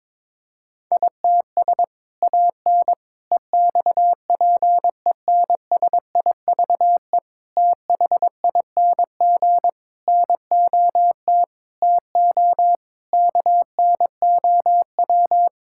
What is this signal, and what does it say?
Morse code, heard as a keyed tone, about 22 wpm, 700 Hz. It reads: ITS AN EXPENSIVE THING NOT TO KNOW
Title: Sentences from Top 1000 Words - Review Part 6 - 22wpm